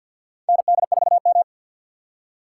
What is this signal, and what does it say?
Morse code, heard as a keyed tone, about 52 wpm, 700 Hz. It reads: ND4M
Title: USA Call Signs - 4 Characters - 6x Character Spacing - 50wpm